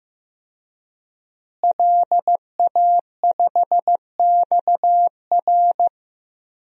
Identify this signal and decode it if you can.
Morse code, heard as a keyed tone, about 16 wpm, 700 Hz. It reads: LA5XR
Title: DXCC-100 Call Signs - 5 Characters - Repeated 3x - 15wpm